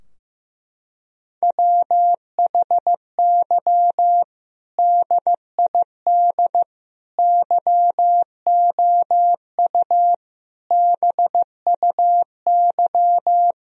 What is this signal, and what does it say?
Morse code, heard as a keyed tone, about 15 wpm, 700 Hz. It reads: WHY DID YOU BUY
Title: Sentences from Top 1000 Words - Review Part 4 - 15wpm